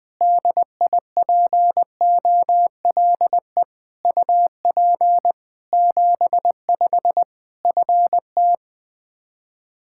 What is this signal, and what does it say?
Morse code, heard as a keyed tone, about 21 wpm, 700 Hz. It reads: DIPOLE UP 75 FT
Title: Practice QSO - Exchange 2 - Info - 20wpm